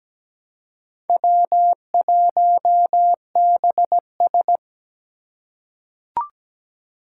Morse code, heard as a keyed tone, about 17 wpm, 700 Hz. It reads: W1BS
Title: Four Challenging Letter-Number Combinations 17wpm